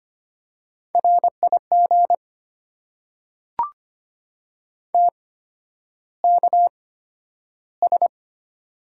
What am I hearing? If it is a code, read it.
Morse code, heard as a keyed tone, about 25 wpm, spaced as for 11 wpm, 700 Hz. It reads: RIG TKS
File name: Top 90 QSO Elements - 8x Character Spacing - 25wpm